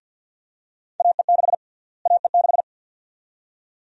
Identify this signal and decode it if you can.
Morse code, heard as a keyed tone, about 53 wpm, 700 Hz. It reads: AE6 AE6
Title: USA Call Sign Prefixes - Repeated 2x - 50wpm